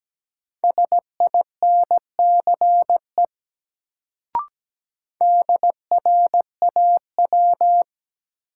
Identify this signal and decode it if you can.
Morse code, heard as a keyed tone, about 17 wpm, 700 Hz. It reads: SINCE DRAW
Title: Sentences from Top 300 Words - New Word Drill - 17wpm